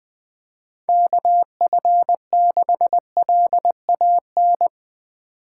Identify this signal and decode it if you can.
Morse code, heard as a keyed tone, about 20 wpm, 700 Hz. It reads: KF6LAN
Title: USA Call Signs - 6 Characters - 2x Character Spacing - 20wpm